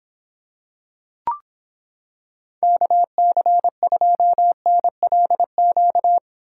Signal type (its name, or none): Morse code